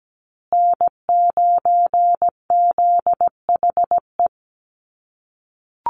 Morse code, heard as a keyed tone, about 17 wpm, 700 Hz. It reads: N9ZHE
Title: USA Call Signs Repeated 2x 17wpm